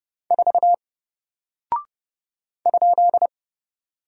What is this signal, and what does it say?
Morse code, heard as a keyed tone, about 30 wpm, 700 Hz. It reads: 4 ?
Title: Single Character (Letter:Number:Punctuation:Prosign) Alt 30wpm